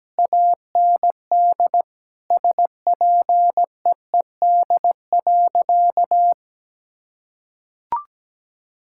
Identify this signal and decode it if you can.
Morse code, heard as a keyed tone, about 17 wpm, 700 Hz. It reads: AND SPEED.
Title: Sentences from Top 1000 Words - Review Part 5 - 17wpm